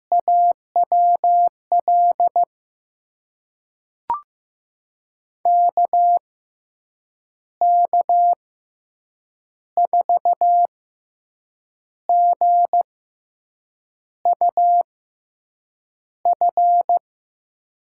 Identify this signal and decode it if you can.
Morse code, heard as a keyed tone, about 15 wpm, spaced as for 6 wpm, 700 Hz. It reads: AWL KK4GUF
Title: USA Call Signs - 6 Characters - 6x Character Spacing - 15wpm